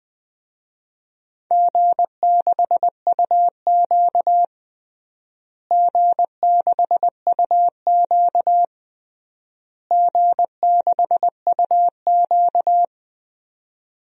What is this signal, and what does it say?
Morse code, heard as a keyed tone, about 20 wpm, 700 Hz. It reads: G6UQ G6UQ G6UQ
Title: DXCC-100 Call Signs - 4 Characters - Repeated 3x - 20wpm